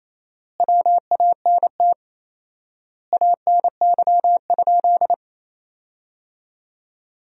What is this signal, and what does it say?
Morse code, heard as a keyed tone, about 28 wpm, 700 Hz. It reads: WANT ANY?
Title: 4x Word-Spacing - Sentences from Top 300 Words - 28wpm